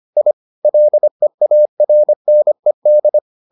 Morse code, heard as a keyed tone, about 25 wpm, 600 Hz. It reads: I LEARNED